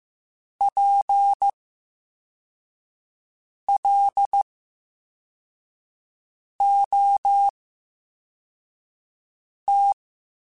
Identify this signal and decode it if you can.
Morse code, heard as a keyed tone, about 15 wpm, spaced as for 4 wpm, 800 Hz. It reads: PLOT
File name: Special INGs / Special INGs-mp3s / Slow ing @13